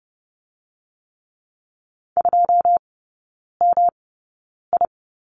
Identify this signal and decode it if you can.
Morse code, heard as a keyed tone, about 30 wpm, spaced as for 9 wpm, 700 Hz. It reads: 2MI
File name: Lesson 18e - Letter m - Three Characters 30wpm